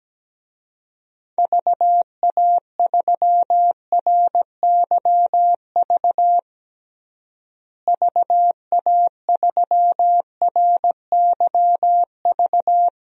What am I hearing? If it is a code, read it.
Morse code, heard as a keyed tone, about 17 wpm, 700 Hz. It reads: VA3RYV VA3RYV
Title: DXCC-100 Call Signs - 6 Characters - Repeated 2x - 17wpm